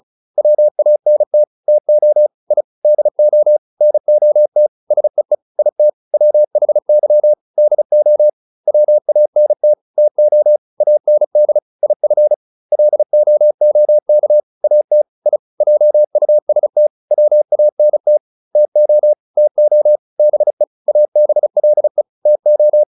Morse code, heard as a keyed tone, about 34 wpm, 600 Hz. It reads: WANT TO I DO NOT SEE IT WHY DO WANT TO AND IF LOOK AT I JUST WANT TO TO BE ABLE TO